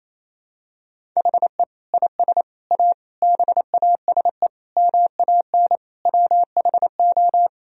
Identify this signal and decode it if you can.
Morse code, heard as a keyed tone, about 27 wpm, 700 Hz. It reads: HE IS A BASE MAN WHO
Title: Sentences from Top 1000 Words - Review Part 1 - 28wpm